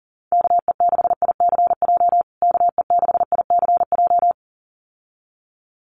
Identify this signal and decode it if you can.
Morse code, heard as a keyed tone, about 40 wpm, 700 Hz. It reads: KE6ICJ KE6ICJ
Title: USA Call Signs Repeated 2x 40wpm